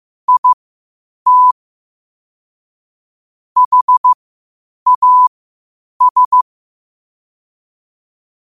Morse code, heard as a keyed tone, about 15 wpm, spaced as for 8 wpm, 1000 Hz. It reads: IT HAS
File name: qotdfiles / 8wpm